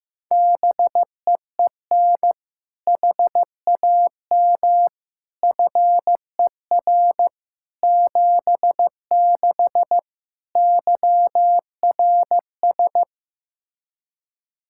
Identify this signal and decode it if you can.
Morse code, heard as a keyed tone, about 16 wpm, 700 Hz. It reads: BEEN HAM FER 76 YRS
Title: Practice QSO - Exchange 3 - More Info - 15wpm